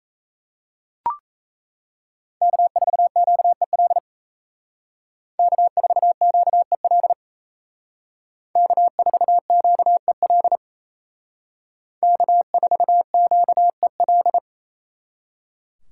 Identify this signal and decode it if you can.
Morse code, heard as a keyed tone, about 36 wpm, 700 Hz. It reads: K4QEL K4QEL K4QEL K4QEL